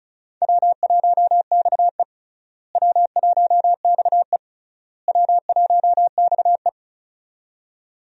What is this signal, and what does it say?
Morse code, heard as a keyed tone, about 36 wpm, 700 Hz. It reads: W1XE W1XE W1XE
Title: USA Call Signs - 4 Characters - Repeated 3x - 35wpm